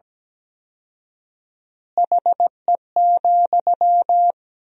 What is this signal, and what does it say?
Morse code, heard as a keyed tone, about 17 wpm, 700 Hz. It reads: HE,
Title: 4x Word-Spacing - Sentences from Top 500 Words - 17wpm